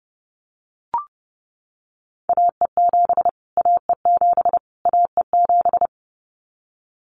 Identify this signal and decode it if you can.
Morse code, heard as a keyed tone, about 30 wpm, 700 Hz. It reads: AE7 AE7 AE7
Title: Callsign Prefixes for Top 25 DXCC - Repeated 3x - 30wpm